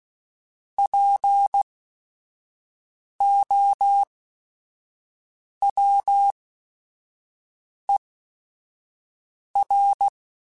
Morse code, heard as a keyed tone, about 16 wpm, spaced as for 5 wpm, 800 Hz. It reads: POWER
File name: Technical Subjects / Technical Subjects-mp3s / Sw Pwr Sup @5